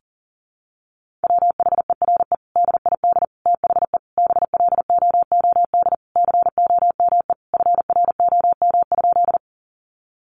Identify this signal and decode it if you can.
Morse code, heard as a keyed tone, about 40 wpm, 700 Hz. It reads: WHERE DID THE BLOOD COME FROM?
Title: Sentences from Top 1100 Words 40wpm